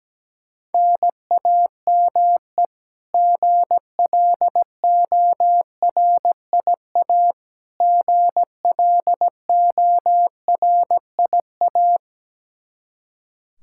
Morse code, heard as a keyed tone, about 17 wpm, 700 Hz. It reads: NAME GLORIA GLORIA
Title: K1USN SST QSO Practice 17wpm